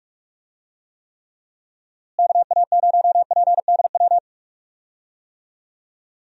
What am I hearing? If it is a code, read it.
Morse code, heard as a keyed tone, about 41 wpm, 700 Hz. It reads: KA0PDW
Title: USA Call Signs - 6 Characters - Repeated 3x - 45wpm